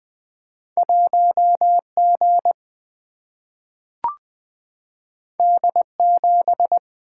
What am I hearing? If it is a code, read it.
Morse code, heard as a keyed tone, about 20 wpm, 700 Hz. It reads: 1 G D 7
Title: Single-Letter Single-Number 20wpm